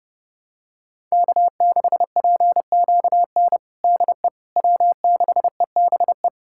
Morse code, heard as a keyed tone, about 30 wpm, 700 Hz. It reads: K6PQN DE W6EBE
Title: USA Callsign DE Callsign - 5 Characters - 30wpm